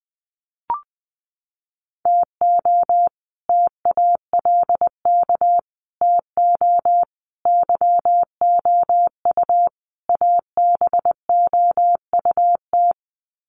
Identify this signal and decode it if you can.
Morse code, heard as a keyed tone, about 20 wpm, 700 Hz. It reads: TO TALK TO YOU ABOUT
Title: Sets of 5 Words Encore 20wpm